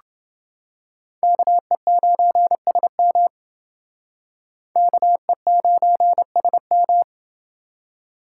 Morse code, heard as a keyed tone, about 28 wpm, 700 Hz. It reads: KE9SM KE9SM